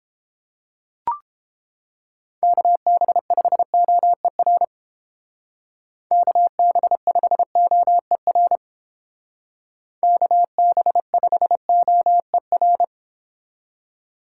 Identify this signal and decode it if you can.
Morse code, heard as a keyed tone, about 29 wpm, 700 Hz. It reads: KB5OER KB5OER KB5OER